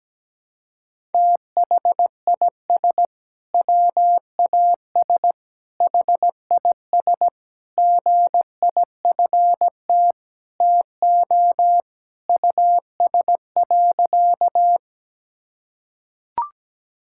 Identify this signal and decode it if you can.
Morse code, heard as a keyed tone, about 17 wpm, 700 Hz. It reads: THIS WAS HIS GIFT TO US.
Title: Sentences from Top 1000 Words - Review Part 6 - 17wpm